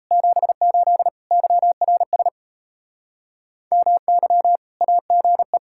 Morse code, heard as a keyed tone, about 34 wpm, 700 Hz. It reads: 78 YRS MY AGE
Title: Speed-Racing Common QSO Phrases - 28wpm